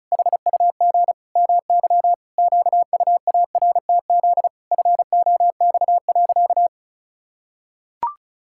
Morse code, heard as a keyed tone, about 36 wpm, 700 Hz. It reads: HUG MY QUARTZ FOX.
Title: Pangram Challenge v2 - 35wpm